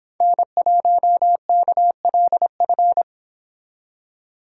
Morse code, heard as a keyed tone, about 26 wpm, 700 Hz. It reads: N1KLF